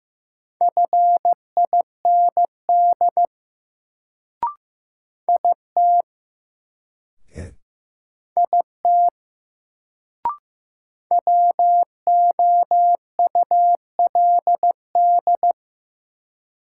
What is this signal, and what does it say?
Morse code, heard as a keyed tone, about 15 wpm, 700 Hz. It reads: FIND IT IT WOULD
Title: Sentences from Top 100 Words - New Word Drill - 15wpm